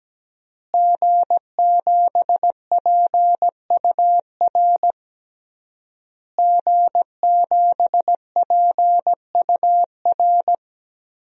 Morse code, heard as a keyed tone, about 17 wpm, 700 Hz. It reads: G7PUR G7PUR
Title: DXCC-100 Call Signs - 5 Characters - Repeated 3x - 17wpm